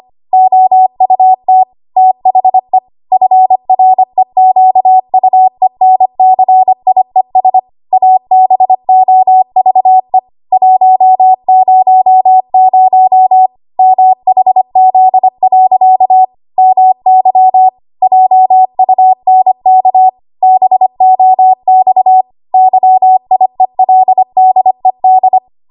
Morse code, heard as a keyed tone, about 25 wpm, 750 Hz. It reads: OUT THE FREQUENCIES ABOVE 100 MHZ. MY JUNK BOX YIELDED
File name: Archive / 25 WPM / 25WPM